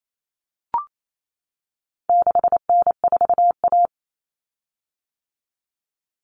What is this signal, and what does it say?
Morse code, heard as a keyed tone, about 28 wpm, 700 Hz. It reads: 6N4A
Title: Common Cut Numbers - 4-Digit 28wpm